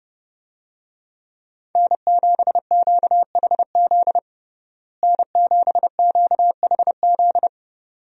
Morse code, heard as a keyed tone, about 32 wpm, 700 Hz. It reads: N7QHZ N7QHZ
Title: USA Call Signs - 5 Characters - Repeated 2x - 30wpm